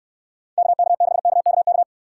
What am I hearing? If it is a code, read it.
Morse code, heard as a keyed tone, about 68 wpm, 700 Hz. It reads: KK6KKK